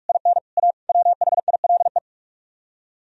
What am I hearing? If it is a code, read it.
Morse code, heard as a keyed tone, about 48 wpm, 700 Hz. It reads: IN A WHILE